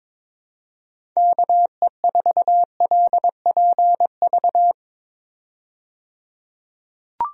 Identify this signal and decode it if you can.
Morse code, heard as a keyed tone, about 22 wpm, 700 Hz. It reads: KE4LPV
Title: USA Call Signs - 6 Characters - 22wpm